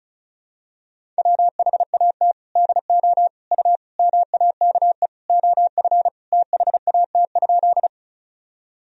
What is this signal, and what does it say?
Morse code, heard as a keyed tone, about 36 wpm, 700 Hz. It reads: WHAT DO U MAKE OF THAT?
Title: Sentences from Top 100 Words and Abbreviations 35wpm